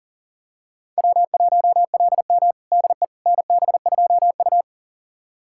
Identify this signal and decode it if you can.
Morse code, heard as a keyed tone, about 40 wpm, 700 Hz. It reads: W1LM DE NB2U
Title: USA Callsign DE Callsign - 4 Characters - 40wpm